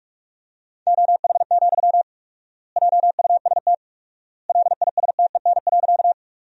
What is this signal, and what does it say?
Morse code, heard as a keyed tone, about 47 wpm, 700 Hz. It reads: OH, JUST LISTEN.
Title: 4x Word-Spacing - Sentences from Top 500 Words - 45wpm